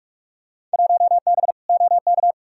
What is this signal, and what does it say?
Morse code, heard as a keyed tone, about 42 wpm, 700 Hz. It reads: 1B OK
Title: Field Day Exchanges Encore - 45wpm